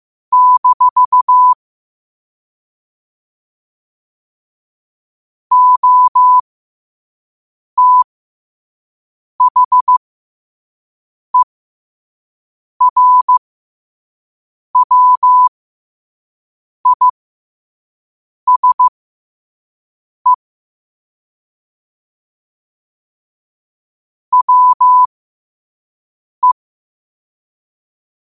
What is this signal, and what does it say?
Morse code, heard as a keyed tone, about 15 wpm, spaced as for 5 wpm, 1000 Hz. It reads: - OTHERWISE WE
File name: qotdfiles / 5wpm